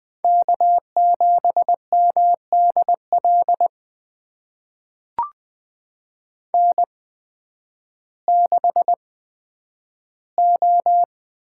Morse code, heard as a keyed tone, about 20 wpm, spaced as for 11 wpm, 700 Hz. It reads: K7MDL N6O
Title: USA Call Signs - 8x Character Spacing - 20wpm